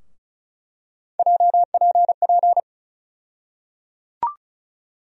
Morse code, heard as a keyed tone, about 36 wpm, 700 Hz. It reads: JPP
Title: USA Call Sign Postfixes - 3 Letters 35wpm